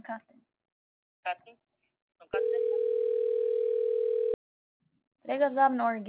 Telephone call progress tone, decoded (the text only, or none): ringback tone